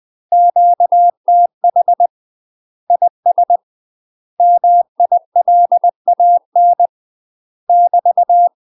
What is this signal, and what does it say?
Morse code, heard as a keyed tone, about 20 wpm, 700 Hz. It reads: QTH IS MILAN <BT>
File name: UK MASTER QSOs TRAINING / 20WPM / UK MASTER QSOs 0071